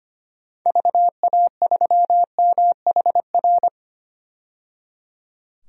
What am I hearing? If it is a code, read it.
Morse code, heard as a keyed tone, about 25 wpm, 700 Hz. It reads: VA3MHR
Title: DXCC-100 Call Signs - 6 Characters - 25wpm